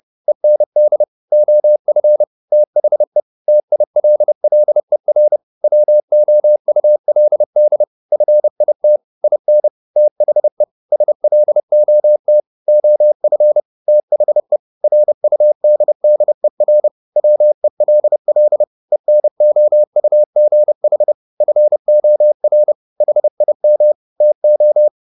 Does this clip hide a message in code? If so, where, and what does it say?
Yes: Morse code from 0.3 to 24.9 s, heard as a keyed tone, about 30 wpm, 600 Hz. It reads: END OF THE TILLER WOULD FIT IN THE SLOT OF THE RUDDER WELL ENOUGH FOR HIM TO